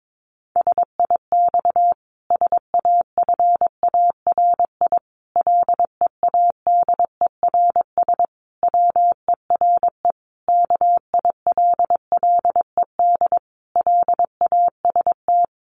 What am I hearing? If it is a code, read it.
Morse code, heard as a keyed tone, about 22 wpm, 700 Hz. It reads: SIX SAFARI LEADERS WERE KILLED LAST
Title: A Sound Of Thunder 22wpm